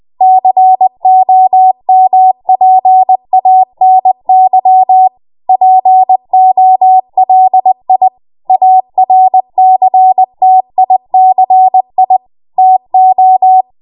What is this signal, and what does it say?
Morse code, heard as a keyed tone, about 20 wpm, 750 Hz. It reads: COMPANY POLI ARCTICI TO
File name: Morse / 20WPM